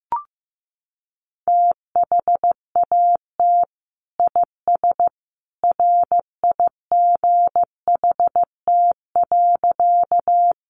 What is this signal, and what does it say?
Morse code, heard as a keyed tone, about 15 wpm, 700 Hz. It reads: THAT IS RIGHT.
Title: Sentences from Top 200 Words 15wpm